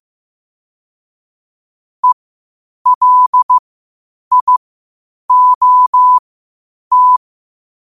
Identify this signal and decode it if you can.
Morse code, heard as a keyed tone, about 15 wpm, spaced as for 8 wpm, 1000 Hz. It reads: ELIOT
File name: qotdfiles / 8wpm